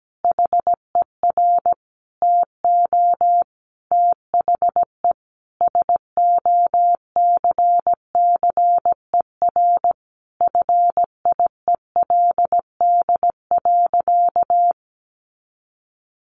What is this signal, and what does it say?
Morse code, heard as a keyed tone, about 17 wpm, 700 Hz. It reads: HER TO THE SOCCER FIELD.
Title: Easy Phrases 17wpm